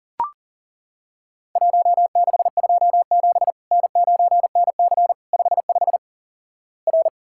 Morse code, heard as a keyed tone, about 37 wpm, 700 Hz. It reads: 1627 N9NC 55 R
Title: Worked All Europe DX Contest - QTC Exchange - 40wpm